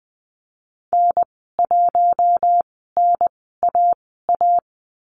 Morse code, heard as a keyed tone, about 20 wpm, spaced as for 14 wpm, 700 Hz. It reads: N1NAA
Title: 088 Lesson 15g - Number 5 - Call Signs 20wpm